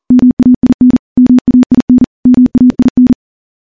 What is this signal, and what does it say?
Morse code, heard as a keyed tone, about 40 wpm, 250 Hz. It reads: MAIN MAIN MAIN
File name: morse / morse-training-40WPM